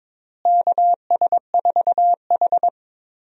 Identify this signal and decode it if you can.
Morse code, heard as a keyed tone, about 22 wpm, 700 Hz. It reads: KS4H
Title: USA Call Signs - 4 Characters - Repeated 3x - 22wpm